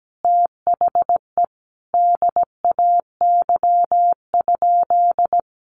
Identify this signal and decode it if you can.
Morse code, heard as a keyed tone, about 17 wpm, 700 Hz. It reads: THE DAY?
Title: Sentences from Top 1100 Words 17wpm